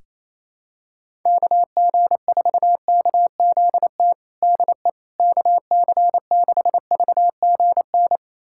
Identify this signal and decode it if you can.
Morse code, heard as a keyed tone, about 28 wpm, 700 Hz. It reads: KG4KZT DE KC6VGN
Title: USA Callsign DE Callsign 28wpm